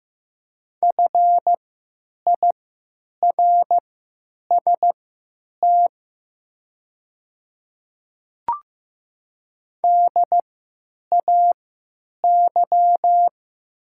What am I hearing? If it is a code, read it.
Morse code, heard as a keyed tone, about 15 wpm, spaced as for 9 wpm, 700 Hz. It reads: FIRST DAY
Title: 3x Character Spacing - Top 100 Words - 15wpm